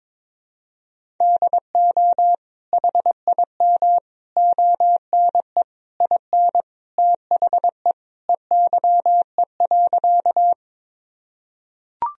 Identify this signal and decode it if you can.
Morse code, heard as a keyed tone, about 22 wpm, 700 Hz. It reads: DO HIM ONE IN THE EYE.
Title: Sentences from Top 1000 Words - Review Part 1 - 22wpm